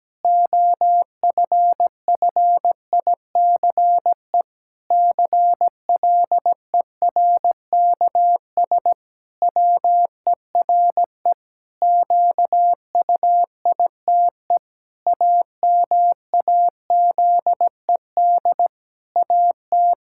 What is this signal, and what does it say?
Morse code, heard as a keyed tone, about 17 wpm, 700 Hz. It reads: OFFICE CLERKS WERE QUITE AMAZED AT
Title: Pangram Challenge v2 - 17wpm